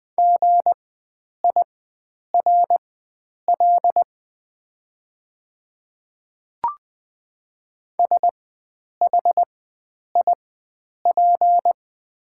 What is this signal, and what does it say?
Morse code, heard as a keyed tone, about 20 wpm, spaced as for 9 wpm, 700 Hz. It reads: GIRL SHIP
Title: 4x Character Spacing - Top 500 Words - 20wpm